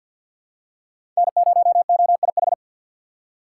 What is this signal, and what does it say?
Morse code, heard as a keyed tone, about 46 wpm, 700 Hz. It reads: N0OIH
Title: USA Call Signs - 5 Characters - 3x Character Spacing - 50wpm